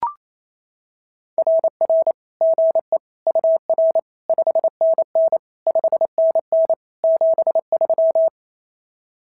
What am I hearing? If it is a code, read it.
Morse code, heard as a keyed tone, about 28 wpm, 650 Hz. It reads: RR GE UR 5NN 5NN 73
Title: SOTA QSO Practice 28wpm